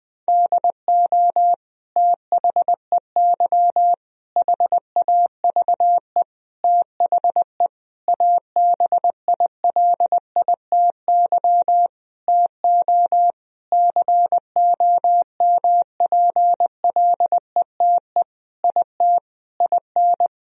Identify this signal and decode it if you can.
Morse code, heard as a keyed tone, about 20 wpm, 700 Hz. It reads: DO THEY HAVE THE ABILITY TO COMPLETE IT IN
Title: Sentences from Top 1000 Words - Review Part 3 - 20wpm